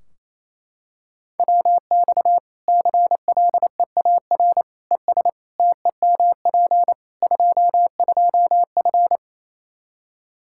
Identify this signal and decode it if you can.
Morse code, heard as a keyed tone, about 28 wpm, 700 Hz. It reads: WX CLEAR ES TEMP 22F
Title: Practice QSO - Exchange 2 - Info - 28wpm